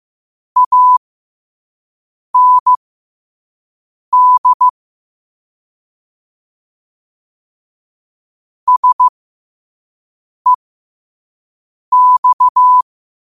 Morse code, heard as a keyed tone, about 15 wpm, spaced as for 5 wpm, 1000 Hz. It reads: AND SEX
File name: qotdfiles / 5wpm